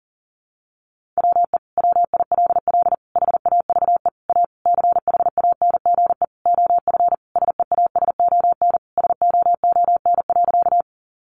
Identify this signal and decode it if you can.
Morse code, heard as a keyed tone, about 40 wpm, 700 Hz. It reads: WE WILL HAVE A CHANGE OF SEASON SOON.
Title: Sentences from Top 1400 Words 40wpm